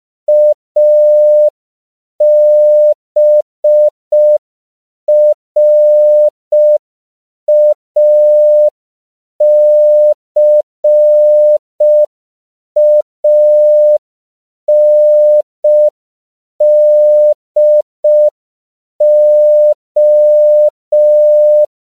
Morse code, heard as a keyed tone, about 5 wpm, 600 Hz. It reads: ABRACANDO
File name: 5PPM